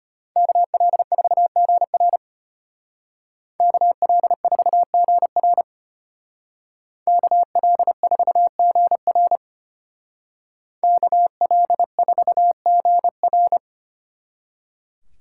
Morse code, heard as a keyed tone, about 33 wpm, 700 Hz. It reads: KL4GR KL4GR KL4GR KL4GR